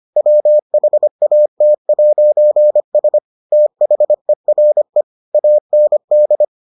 Morse code, heard as a keyed tone, about 25 wpm, 600 Hz. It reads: WHAT'S THERE AND